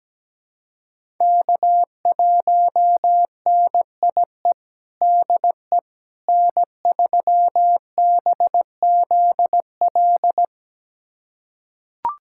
Morse code, heard as a keyed tone, about 17 wpm, 700 Hz. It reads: K1NIE DE N3BZL
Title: USA Callsign DE Callsign - 5 Characters - 17wpm